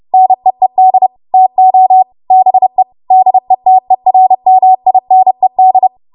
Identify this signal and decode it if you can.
Morse code, heard as a keyed tone, about 30 wpm, 750 Hz. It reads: NEED TO BE DETERMINED
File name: Morse / 30WPM